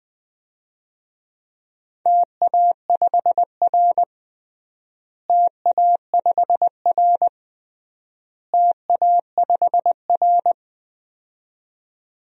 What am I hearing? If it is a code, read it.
Morse code, heard as a keyed tone, about 21 wpm, 700 Hz. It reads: TA5R TA5R TA5R
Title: DXCC-100 Call Signs - 4 Characters - Repeated 3x - 20wpm